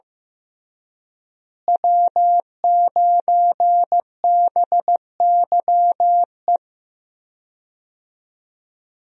Morse code, heard as a keyed tone, about 16 wpm, 700 Hz. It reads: W9BYE
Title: USA Call Signs - 5 Characters - Repeated 3x - 15wpm